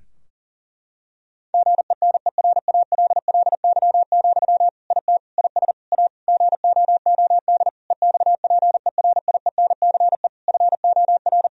Morse code, heard as a keyed tone, about 40 wpm, 700 Hz. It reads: GENERALLY, IT IS A GOOD EXPERIENCE FOR
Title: Sentences from Top 1000 Words - Review Part 4 - 40wpm